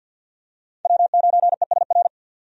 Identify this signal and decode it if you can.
Morse code, heard as a keyed tone, about 50 wpm, 700 Hz. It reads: W9ESR
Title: USA Call Signs - 5 Characters - 50wpm